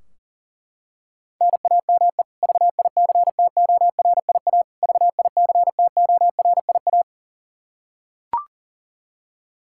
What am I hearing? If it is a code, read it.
Morse code, heard as a keyed tone, about 40 wpm, 700 Hz. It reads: NAME VICTORIA VICTORIA
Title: K1USN SST QSO Practice 40wpm